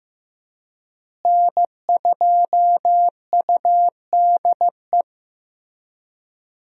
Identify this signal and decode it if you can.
Morse code, heard as a keyed tone, about 15 wpm, 700 Hz. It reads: N2UDE
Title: USA Call Signs - 5 Characters - Repeated 3x - 15wpm